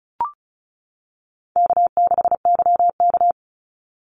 Morse code, heard as a keyed tone, about 34 wpm, 700 Hz. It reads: K6YK
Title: NA SOTA Chaser Callsigns 35wpm